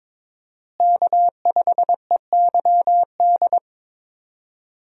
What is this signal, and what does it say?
Morse code, heard as a keyed tone, about 22 wpm, 700 Hz. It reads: K5EYD
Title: USA Call Signs - 5 Characters - Repeated 2x - 22wpm